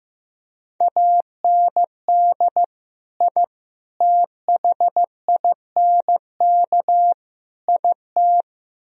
Morse code, heard as a keyed tone, about 15 wpm, 700 Hz. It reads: AND I THINK IT
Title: Speed-Racing - Sets of 4 Words 15wpm